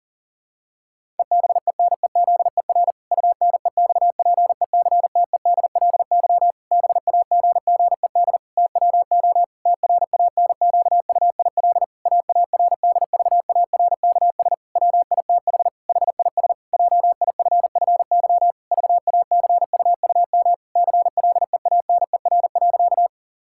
Morse code, heard as a keyed tone, about 40 wpm, 700 Hz. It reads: EBENEZER UNEXPECTEDLY BAGGED TWO TRANQUIL AARDVARKS WITH HIS JIFFY VACUUM CLEANER.
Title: Pangram Challenge v2 - 40wpm